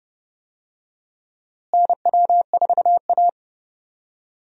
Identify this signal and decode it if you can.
Morse code, heard as a keyed tone, about 31 wpm, 700 Hz. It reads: NW4A
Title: USA Call Signs - 4 Characters - Repeated 3x - 30wpm